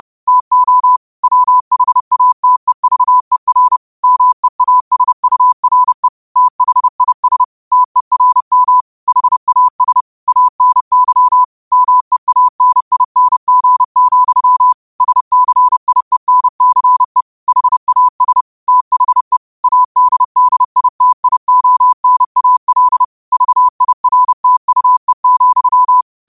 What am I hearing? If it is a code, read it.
Morse code, heard as a keyed tone, about 30 wpm, 1000 Hz. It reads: TO WHATEVER MEASURE THIS TERM HAS ANY MEANING, SCIENCE HAS THE ADDITIONAL VIRTUE,
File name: qotdfiles / 30wpm